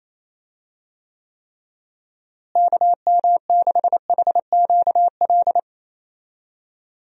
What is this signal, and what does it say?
Morse code, heard as a keyed tone, about 28 wpm, 700 Hz. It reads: KM6HQL
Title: USA Call Signs - 6 Characters - Repeated 3x - 28wpm